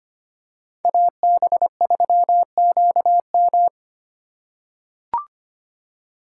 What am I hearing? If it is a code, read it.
Morse code, heard as a keyed tone, about 25 wpm, 700 Hz. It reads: AB3QM
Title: USA Call Signs - 5 Characters - 4x Character Spacing - 25wpm